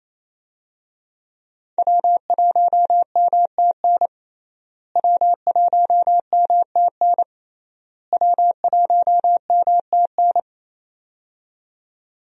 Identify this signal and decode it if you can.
Morse code, heard as a keyed tone, about 28 wpm, 700 Hz. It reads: W1MTN W1MTN W1MTN
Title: USA Call Signs - 5 Characters - Repeated 3x - 28wpm